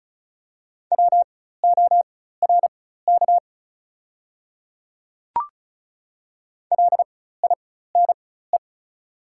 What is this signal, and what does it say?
Morse code, heard as a keyed tone, about 34 wpm, spaced as for 16 wpm, 700 Hz. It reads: WORK LINE
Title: 4x Character Spacing - Top 200 Words - 35wpm